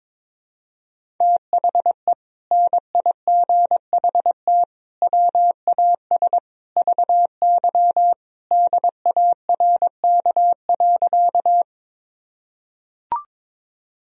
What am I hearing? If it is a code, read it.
Morse code, heard as a keyed tone, about 22 wpm, 700 Hz. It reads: THE NIGHT WAS VY DARK.
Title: Sentences from Top 500 Words and Abbreviations 22wpm